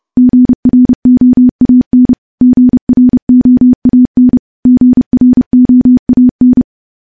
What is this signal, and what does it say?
Morse code, heard as a keyed tone, about 30 wpm, 250 Hz. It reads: GROAN GROAN GROAN
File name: morse / morse-training-30WPM